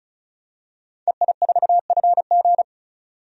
Morse code, heard as a keyed tone, about 36 wpm, 700 Hz. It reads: EI4FG
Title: DXCC-100 Call Signs - 5 Characters - 35wpm